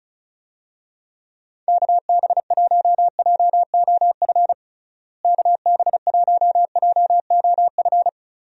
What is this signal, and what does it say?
Morse code, heard as a keyed tone, about 36 wpm, 700 Hz. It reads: KB1JOF KB1JOF
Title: USA Call Signs - 6 Characters - Repeated 2x - 35wpm